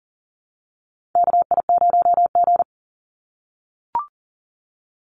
Morse code, heard as a keyed tone, about 40 wpm, 700 Hz. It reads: KI0G
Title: NA SOTA Chaser Callsigns 40wpm